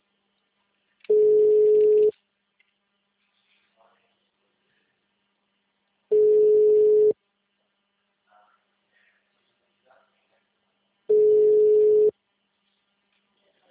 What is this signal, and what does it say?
Ringback tone (Europe, 425 Hz).
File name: out- -2011- - - .10